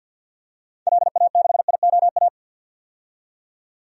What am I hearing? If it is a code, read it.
Morse code, heard as a keyed tone, about 51 wpm, 700 Hz. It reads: LA6IOA
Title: DXCC-100 Call Signs - 6 Characters - 50wpm